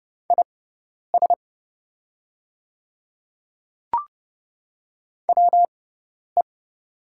Morse code, heard as a keyed tone, about 30 wpm, spaced as for 10 wpm, 690 Hz. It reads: IS WE
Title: 6x Character Spacing - Top 100 Words - 30wpm